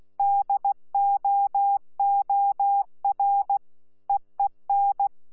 Morse code, heard as a keyed tone, about 16 wpm, 800 Hz. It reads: DOOR EEN
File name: Z PI4VRZ / PI4VRZ 14-jun-2025